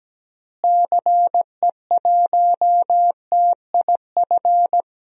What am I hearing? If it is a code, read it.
Morse code, heard as a keyed tone, about 17 wpm, 700 Hz. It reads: CE1TIF
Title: DXCC-100 Call Signs - 6 Characters - Repeated 2x - 17wpm